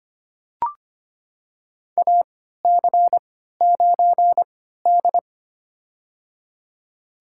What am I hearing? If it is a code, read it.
Morse code, heard as a keyed tone, about 25 wpm, spaced as for 14 wpm, 700 Hz. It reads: AC9D
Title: USA Call Signs - 4 Characters - 3x Character Spacing - 25wpm